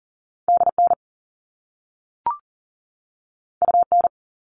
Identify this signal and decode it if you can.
Morse code, heard as a keyed tone, about 42 wpm, 700 Hz. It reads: DN UN
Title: All Cut Numbers - 2-Digit 40wpm